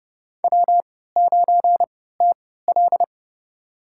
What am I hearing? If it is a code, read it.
Morse code, heard as a keyed tone, about 32 wpm, spaced as for 17 wpm, 700 Hz. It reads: W9TL
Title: USA Call Signs - 3x Character Spacing - 30wpm